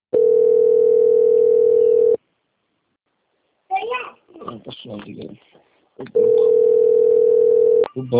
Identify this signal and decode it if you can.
Ringback tone (North America), 440 + 480 Hz, with speech.
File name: out- - - - - .11109